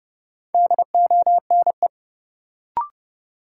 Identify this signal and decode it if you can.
Morse code, heard as a keyed tone, about 30 wpm, 700 Hz. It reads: DONE
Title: Sets of 5 Words Encore - New Word Drill - 30wpm